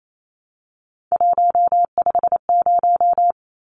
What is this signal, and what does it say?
Morse code, heard as a keyed tone, about 28 wpm, 700 Hz. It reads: 150
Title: Numbers - 3-Digit 28wpm